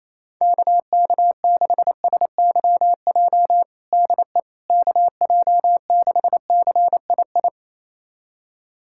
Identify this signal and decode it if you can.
Morse code, heard as a keyed tone, about 28 wpm, 700 Hz. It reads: KK6SYJ DE KJ6CII
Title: USA Callsign DE Callsign - 6 Characters - 28wpm